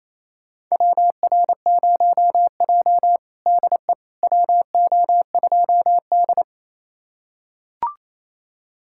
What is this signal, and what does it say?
Morse code, heard as a keyed tone, about 28 wpm, 700 Hz. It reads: WR0J DE WO2D
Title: USA Callsign DE Callsign - 4 Characters - 28wpm